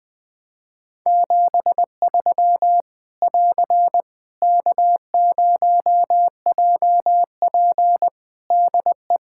Morse code, heard as a keyed tone, about 20 wpm, 700 Hz. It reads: 73 <AR> K0JP DE
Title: Practice QSO - Exchange 4 - Ending - 20wpm